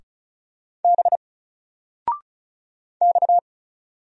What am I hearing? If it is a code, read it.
Morse code, heard as a keyed tone, about 35 wpm, 700 Hz. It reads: B X
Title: Single Letter-Number 35wpm